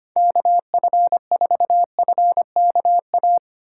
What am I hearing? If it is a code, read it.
Morse code, heard as a keyed tone, about 25 wpm, 700 Hz. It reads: KF4FKA
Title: USA Call Signs - 6 Characters - 3x Character Spacing - 25wpm